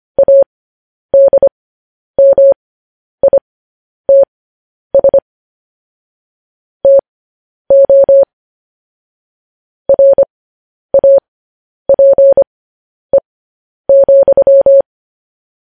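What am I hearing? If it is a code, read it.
Morse code, heard as a keyed tone, about 25 wpm, spaced as for 10 wpm, 550 Hz. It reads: ADMITS TO RAPE,
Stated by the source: Morse code news headlines sent at 10 wpm.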